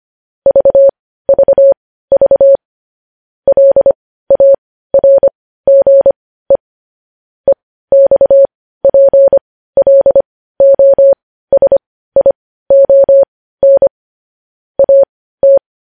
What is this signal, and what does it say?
Morse code, heard as a keyed tone, about 25 wpm, spaced as for 15 wpm, 550 Hz. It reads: VVV LARGE EXPLOSION AT
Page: News headlines in morse code at 15 wpm